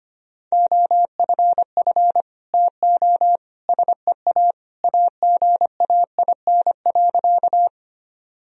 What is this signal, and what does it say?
Morse code, heard as a keyed tone, about 25 wpm, 700 Hz. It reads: OFF TO SEA AGAIN.
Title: Sentences from Top 1000 Words - Review Part 1 - 25wpm